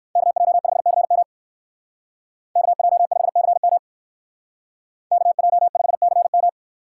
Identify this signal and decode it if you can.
Morse code, heard as a keyed tone, about 58 wpm, 700 Hz. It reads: KJ5CM KJ5CM KJ5CM